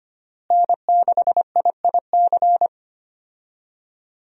Morse code, heard as a keyed tone, about 25 wpm, 700 Hz. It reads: N6IIC
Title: USA Call Signs - 5 Characters - Repeated 3x - 25wpm